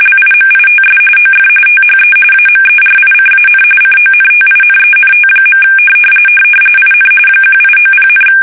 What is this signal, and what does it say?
RTTY, about 45.45 baud. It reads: LIMITS OF THE AMERICAN CONTINENT. IN 1890,
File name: fsk850 / usnavy3